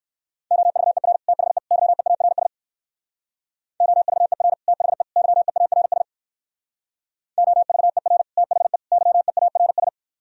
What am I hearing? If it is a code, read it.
Morse code, heard as a keyed tone, about 58 wpm, 700 Hz. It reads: OVER THE YEARS OVER THE YEARS OVER THE YEARS